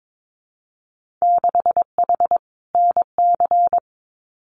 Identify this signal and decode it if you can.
Morse code, heard as a keyed tone, about 22 wpm, 700 Hz. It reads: 6H NC
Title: Winter Field Day Exchanges Encore 22wpm